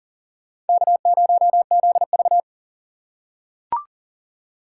Morse code, heard as a keyed tone, about 40 wpm, 700 Hz. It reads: K0ZV
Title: USA Call Signs - 4 Characters - 8x Character Spacing - 40wpm